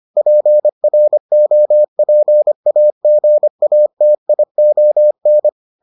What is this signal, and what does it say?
Morse code, heard as a keyed tone, about 25 wpm, 600 Hz. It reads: PROPAGATION